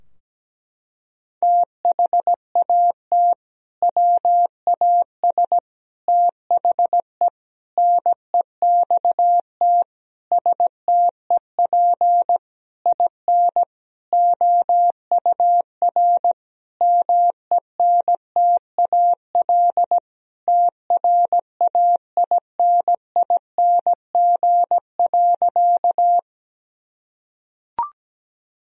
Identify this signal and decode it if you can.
Morse code, heard as a keyed tone, about 17 wpm, 700 Hz. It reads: THAT WAS THE NEXT STEP IN OUR MENTAL TRAINING.
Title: Sentences from Top 1000 Words - Review Part 4 - 17wpm